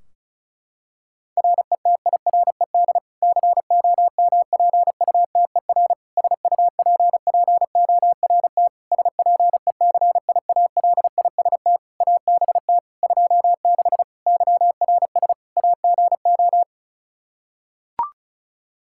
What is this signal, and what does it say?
Morse code, heard as a keyed tone, about 34 wpm, 700 Hz. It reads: RETIRED COMPUTER SUPPORT SPECIALIST ABT 26 YRS AGO
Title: Practice QSO - Exchange 3 - More Info - 35wpm